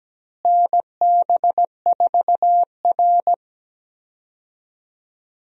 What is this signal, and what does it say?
Morse code, heard as a keyed tone, about 17 wpm, 700 Hz. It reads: NB4R
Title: USA Call Signs - 4 Characters - Repeated 2x - 17wpm